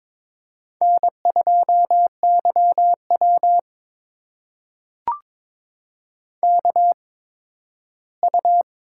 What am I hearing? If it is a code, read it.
Morse code, heard as a keyed tone, about 22 wpm, spaced as for 13 wpm, 700 Hz. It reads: N2YW KU
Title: USA Call Signs - 4 Characters - 8x Character Spacing - 22wpm